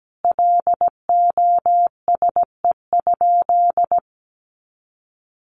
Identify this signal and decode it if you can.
Morse code, heard as a keyed tone, about 17 wpm, 700 Hz. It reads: LOSE?
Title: Sentences from Top 1100 Words 17wpm